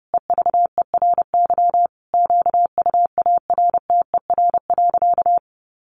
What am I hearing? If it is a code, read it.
Morse code, heard as a keyed tone, about 30 wpm, 700 Hz. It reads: EVERY QUARTER.
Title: Sentences from Top 1500 Words 30wpm